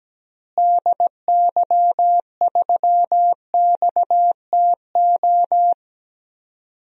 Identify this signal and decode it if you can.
Morse code, heard as a keyed tone, about 17 wpm, 700 Hz. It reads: DY3XTO
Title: DXCC-100 Call Signs - 6 Characters - Repeated 3x - 17wpm